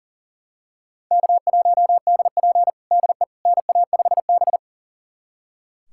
Morse code, heard as a keyed tone, about 40 wpm, 700 Hz. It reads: K1DP DE NA5B
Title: USA Callsign DE Callsign - 4 Characters - 40wpm